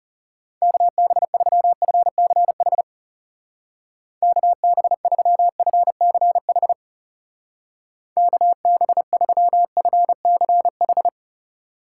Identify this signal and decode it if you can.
Morse code, heard as a keyed tone, about 39 wpm, 700 Hz. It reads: KB3FCH KB3FCH KB3FCH